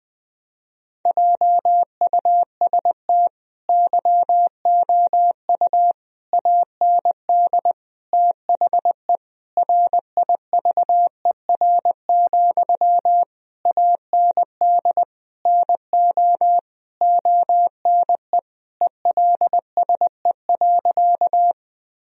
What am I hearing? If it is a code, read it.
Morse code, heard as a keyed tone, about 21 wpm, 700 Hz. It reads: JUST YOU AND THE RIVER, AND NO ONE ELSE.
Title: Sentences from Top 1000 Words - Review Part 1 - 20wpm